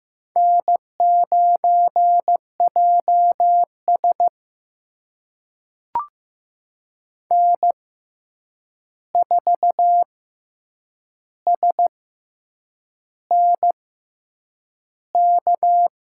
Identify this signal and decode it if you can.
Morse code, heard as a keyed tone, about 15 wpm, spaced as for 7 wpm, 700 Hz. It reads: N9JS N4SNK
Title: USA Call Signs - 6x Character Spacing - 15wpm